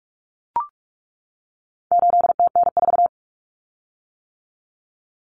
Morse code, heard as a keyed tone, about 44 wpm, 700 Hz. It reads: 8TN4
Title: Common Cut Numbers - 4-Digit 45wpm